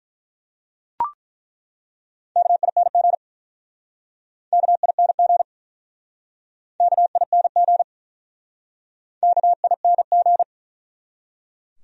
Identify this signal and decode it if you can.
Morse code, heard as a keyed tone, about 48 wpm, 700 Hz. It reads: KING KING KING KING